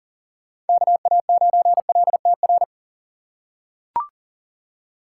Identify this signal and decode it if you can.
Morse code, heard as a keyed tone, about 40 wpm, 700 Hz. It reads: KA9LTR
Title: USA Call Signs - 6 Characters - 4x Character Spacing - 40wpm